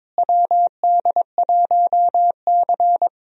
Morse code, heard as a keyed tone, about 22 wpm, 700 Hz. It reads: WD1C
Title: USA Call Signs - 4 Characters - Repeated 2x - 22wpm